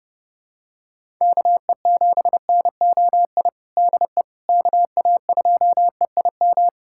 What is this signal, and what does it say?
Morse code, heard as a keyed tone, about 32 wpm, 700 Hz. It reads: KE7NOI DE KA2EIM
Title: USA Callsign DE Callsign - 6 Characters - 30wpm